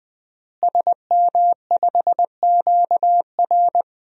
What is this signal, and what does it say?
Morse code, heard as a keyed tone, about 20 wpm, 700 Hz. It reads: SM5QR
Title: DXCC-100 Call Signs - 5 Characters - Repeated 3x - 20wpm